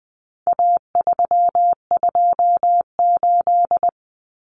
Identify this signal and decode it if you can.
Morse code, heard as a keyed tone, about 20 wpm, 700 Hz. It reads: A328
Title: Common Cut Numbers - 4-Digit 20wpm